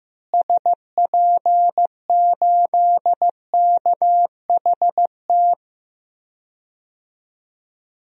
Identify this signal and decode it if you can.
Morse code, heard as a keyed tone, about 16 wpm, 700 Hz. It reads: SP8KHT
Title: DXCC-100 Call Signs - 6 Characters - Repeated 2x - 15wpm